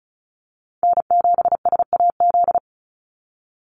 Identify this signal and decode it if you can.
Morse code, heard as a keyed tone, about 34 wpm, 700 Hz. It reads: N7SAZ
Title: USA Call Signs 35wpm